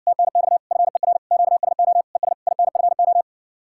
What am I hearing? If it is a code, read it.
Morse code, heard as a keyed tone, about 60 wpm, 700 Hz. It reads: TNX FER QSO ES INFO